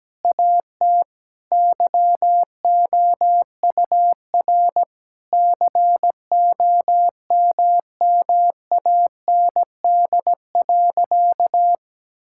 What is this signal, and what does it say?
Morse code, heard as a keyed tone, about 17 wpm, 700 Hz. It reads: AT YOUR COMMAND.
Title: Sentences from Top 1600 Words 17wpm